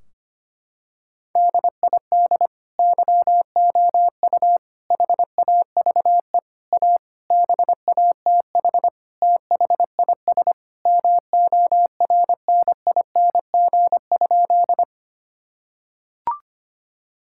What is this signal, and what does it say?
Morse code, heard as a keyed tone, about 25 wpm, 700 Hz. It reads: DID YOU HAVE A BATH THIS MORNING?
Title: Sentences from Top 1600 Words 25wpm